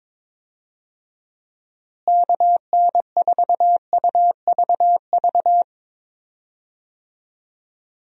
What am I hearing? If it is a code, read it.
Morse code, heard as a keyed tone, about 21 wpm, 700 Hz. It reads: KN4UVV
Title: USA Call Signs - 6 Characters - Repeated 2x - 22wpm